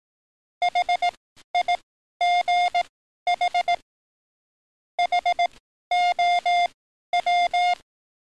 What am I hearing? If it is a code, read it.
Morse code, heard as a keyed tone, about 18 wpm, spaced as for 12 wpm, 710 Hz. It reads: HIGH HOW
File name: training / NAVPERS 11049to11060 / NAVPERS 11053 RA RB